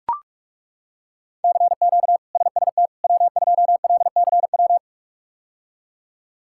Morse code, heard as a keyed tone, about 41 wpm, 700 Hz. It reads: CQ SST W2LCW
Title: K1USN SST QSO Practice - 45wpm